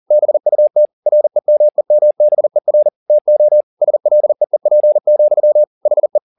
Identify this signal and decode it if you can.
Morse code, heard as a keyed tone, about 40 wpm, 600 Hz. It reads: BUT REMEMBER TO SLEEP, HE
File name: the old man and the sea 2 40wpm